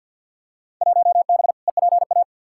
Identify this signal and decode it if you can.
Morse code, heard as a keyed tone, about 50 wpm, 700 Hz. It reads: 1B EPA
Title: Field Day Exchanges Encore - 50wpm